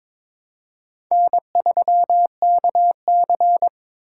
Morse code, heard as a keyed tone, about 22 wpm, 700 Hz. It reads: N3KC
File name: practice / 22wpm / Four Letter-Number Combinations - 22wpm